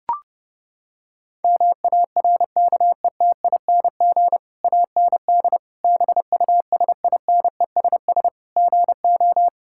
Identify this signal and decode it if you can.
Morse code, heard as a keyed tone, about 32 wpm, 700 Hz. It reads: MARKETING AND BUSINESS GO
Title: Sentences from Top 1000 Words - Review Part 5 - 30wpm